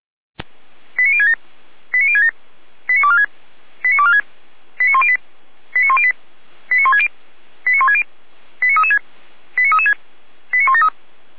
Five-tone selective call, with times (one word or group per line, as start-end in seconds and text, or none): 1.0-1.3 89067
1.9-2.3 89067
2.9-3.3 89246
3.8-4.2 89246
4.8-5.2 89108
5.8-6.1 89108
6.7-7.1 89160
7.7-8.0 89160
8.6-9.0 89336
9.6-9.9 89336
10.5-10.9 89173